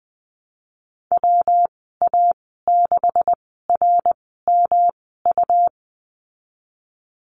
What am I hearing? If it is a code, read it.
Morse code, heard as a keyed tone, about 20 wpm, spaced as for 15 wpm, 700 Hz. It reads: WA6RMU
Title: 140 Lesson 23g - Letter f - Call Signs 20wpm